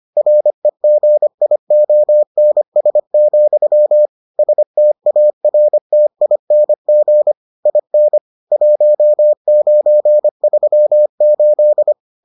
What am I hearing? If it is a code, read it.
Morse code, heard as a keyed tone, about 25 wpm, 600 Hz. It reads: REGIONS, STARTING IN 1938